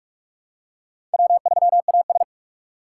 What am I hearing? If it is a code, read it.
Morse code, heard as a keyed tone, about 46 wpm, 700 Hz. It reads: W3AS
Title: USA Call Signs - 4 Characters - 4x Character Spacing - 45wpm